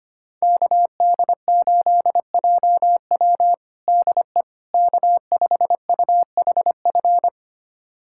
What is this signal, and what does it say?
Morse code, heard as a keyed tone, about 25 wpm, 700 Hz. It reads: KD8JW DE K5UHF
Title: USA Callsign DE Callsign - 5 Characters - 25wpm